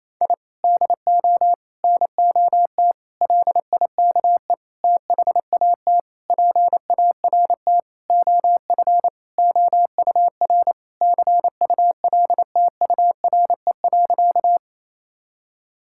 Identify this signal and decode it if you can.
Morse code, heard as a keyed tone, about 28 wpm, 700 Hz. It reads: I DO NOT LIKE THAT PART OF OUR CULTURE.
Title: Sentences from Top 1000 Words - Review Part 6 - 28wpm